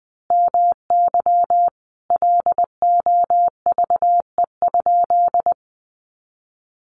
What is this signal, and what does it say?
Morse code, heard as a keyed tone, about 19 wpm, 700 Hz. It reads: MY LOVE?
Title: Sentences from Top 500 Words 20wpm